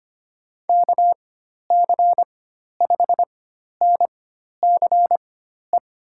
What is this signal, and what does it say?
Morse code, heard as a keyed tone, about 25 wpm, spaced as for 12 wpm, 700 Hz. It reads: KC5NCE
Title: USA Call Signs - 6 Characters - 4x Character Spacing - 25wpm